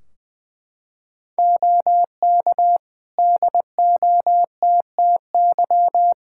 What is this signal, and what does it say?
Morse code, heard as a keyed tone, about 21 wpm, 700 Hz. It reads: OK DOTTY
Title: K1USN SST QSO Practice 20wpm